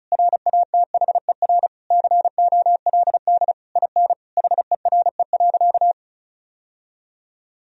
Morse code, heard as a keyed tone, about 34 wpm, 700 Hz. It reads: RATHER COLD IN HERE.
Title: Sentences from Top 1000 Words - Review Part 4 - 35wpm